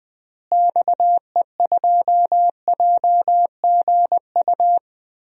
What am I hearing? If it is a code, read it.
Morse code, heard as a keyed tone, about 20 wpm, 700 Hz. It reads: XE2JGU
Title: DXCC-100 Call Signs - 6 Characters - Repeated 2x - 20wpm